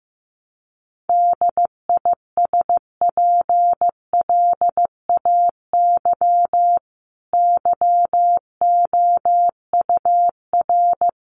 Morse code, heard as a keyed tone, about 15 wpm, 700 Hz. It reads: DISPLAY YOUR
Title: Sentences from Top 1200 Words 15wpm